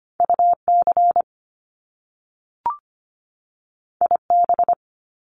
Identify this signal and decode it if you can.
Morse code, heard as a keyed tone, about 25 wpm, 700 Hz. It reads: UC IB
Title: Top Two Letter Combinations in Words 25wpm